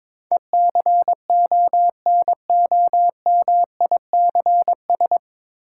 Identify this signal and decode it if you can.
Morse code, heard as a keyed tone, about 22 wpm, 700 Hz. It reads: ECONOMICS
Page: Morse Code Ninja practice 22wpm